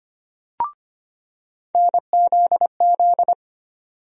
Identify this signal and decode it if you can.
Morse code, heard as a keyed tone, about 25 wpm, 700 Hz. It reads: NZZ
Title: USA Call Sign Postfixes - 3 Letters 25wpm